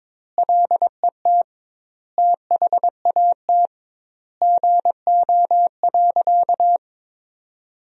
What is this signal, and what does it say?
Morse code, heard as a keyed tone, about 21 wpm, 700 Hz. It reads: LET THAT GO.
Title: 2x Word-Spacing - Sentences from Top 300 Words - 22wpm